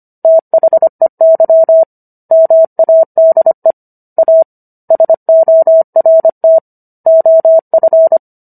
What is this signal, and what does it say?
Morse code, heard as a keyed tone, about 25 wpm, 650 Hz. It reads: THEY MADE A SORT OF